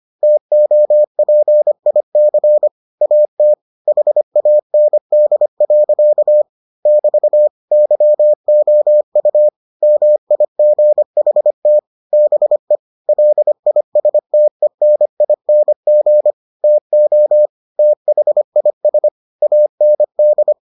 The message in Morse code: TOPIC AT HAND. <BT> YOU MIGHT BE LISTENING TO THIS AND

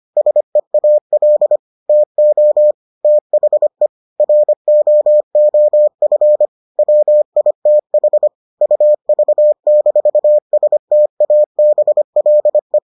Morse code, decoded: SEAL TO THE ROOF WITH UV-STABLE

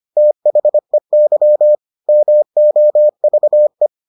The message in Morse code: THEY MOVE

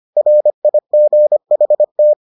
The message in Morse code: RIGHT

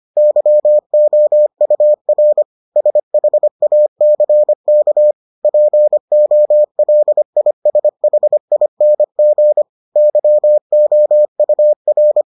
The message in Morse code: YOUR SHACK POLISHING YOUR

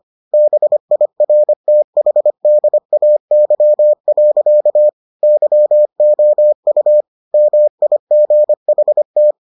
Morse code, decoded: BIRTHDAY. YOU MIGHT

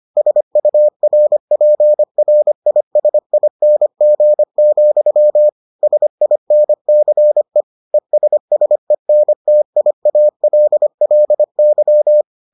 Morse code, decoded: SURPRISING, SINCE ESSENTIALLY

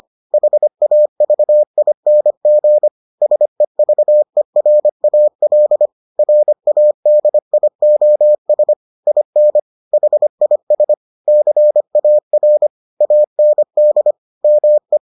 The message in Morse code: HAVING SEVERAL RADIOS IN HIS CAR AND ME

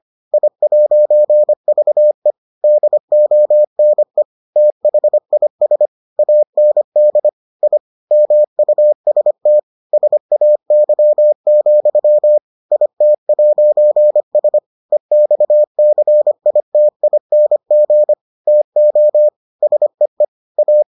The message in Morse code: I'VE DONE THIS AND I MUST SAY, IT'S EXCITING TO SEE A

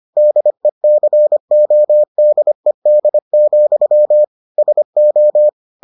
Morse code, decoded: DECODED, SO

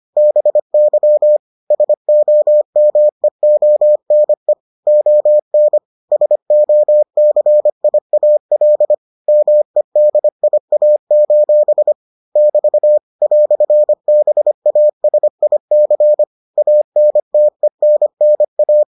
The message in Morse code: BY SOMEONE ON SOCIAL MEDIA: <BT> "BASIC ANTENNA

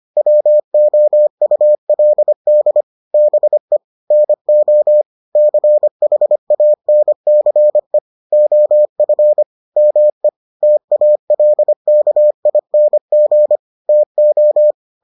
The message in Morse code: WOULD BE NO CHANCE OF ME TALKING TO

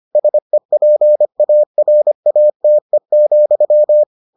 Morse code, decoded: SEPARATE,

